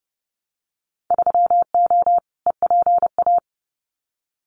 Morse code, heard as a keyed tone, about 30 wpm, 700 Hz. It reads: 3O EPA
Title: Winter Field Day Exchanges 30wpm